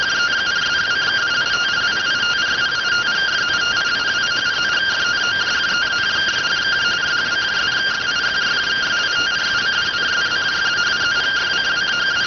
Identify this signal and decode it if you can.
Two-tone teleprinter signal, about 170 Hz shift.